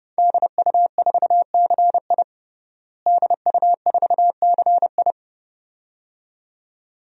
Morse code, heard as a keyed tone, about 30 wpm, 700 Hz. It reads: DU4CI DU4CI
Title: DXCC-100 Call Signs - 5 Characters - Repeated 2x - 30wpm